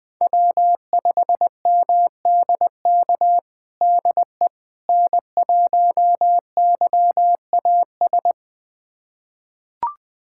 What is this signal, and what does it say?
Morse code, heard as a keyed tone, about 20 wpm, 700 Hz. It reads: W5MDK DE N1YAS
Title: USA Callsign DE Callsign - 5 Characters - 20wpm